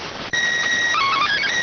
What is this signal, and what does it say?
SSTV (slow-scan television), PD 290.